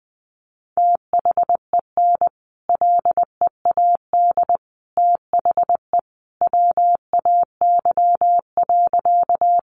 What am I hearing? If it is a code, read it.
Morse code, heard as a keyed tone, about 20 wpm, 700 Hz. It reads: THEN LEAD THE WAY.
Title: Sentences from Top 500 Words 20wpm